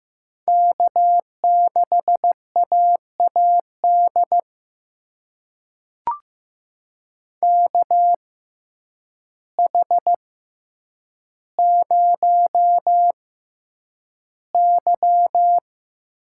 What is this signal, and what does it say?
Morse code, heard as a keyed tone, about 15 wpm, spaced as for 8 wpm, 700 Hz. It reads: K6AAD KH0Y
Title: USA Call Signs - 5 Characters - 6x Character Spacing - 15wpm